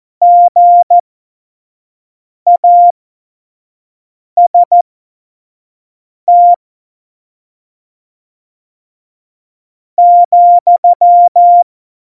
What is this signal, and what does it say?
Morse code, heard as a keyed tone, about 14 wpm, spaced as for 4 wpm, 700 Hz. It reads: GAST,
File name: Morsekurs / Witze / Witz 15. Oktober 2019, 25 F70